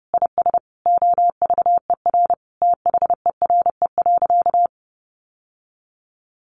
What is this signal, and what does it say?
Morse code, heard as a keyed tone, about 30 wpm, 700 Hz. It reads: IS OVER THERE.
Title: Sentences from Top 1100 Words 30wpm